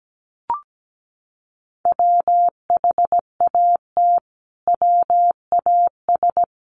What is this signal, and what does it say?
Morse code, heard as a keyed tone, about 17 wpm, 700 Hz. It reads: WHAT WAS
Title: Sets of 5 Words Encore 17wpm